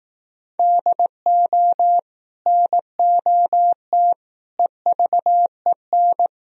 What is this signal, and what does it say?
Morse code, heard as a keyed tone, about 18 wpm, 700 Hz. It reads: DO NOT EVEN
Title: Speed-Racing - Sets of 4 Words 15wpm